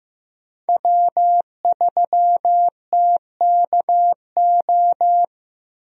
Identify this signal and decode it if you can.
Morse code, heard as a keyed tone, about 15 wpm, 700 Hz. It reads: W3TKO
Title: USA Call Signs - 5 Characters - Repeated 3x - 15wpm